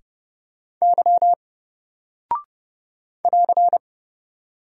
Morse code, heard as a keyed tone, about 30 wpm, 700 Hz. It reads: Y <AR>
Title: Single Character (Letter:Number:Punctuation:Prosign) Alt 30wpm